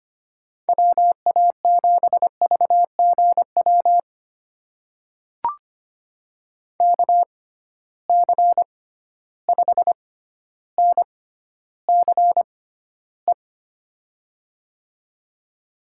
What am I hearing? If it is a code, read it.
Morse code, heard as a keyed tone, about 25 wpm, spaced as for 13 wpm, 700 Hz. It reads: WA7VGW KC5NCE
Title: USA Call Signs - 6 Characters - 6x Character Spacing - 25wpm